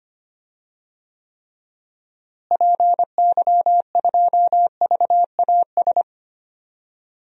Morse code, heard as a keyed tone, about 25 wpm, 700 Hz. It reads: PY2VAS
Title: DXCC-100 Call Signs - 6 Characters - Repeated 2x - 25wpm